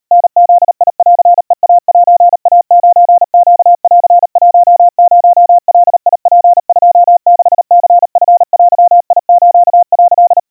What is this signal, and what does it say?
Morse code (Japanese Wabun code), heard as a keyed tone, about 39 wpm, 700 Hz. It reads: タブンヘイセイ9ネン10ガツ26ニチデス」
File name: kawa / Morse sample / kamyu 140 cpm 700 Hz Crispy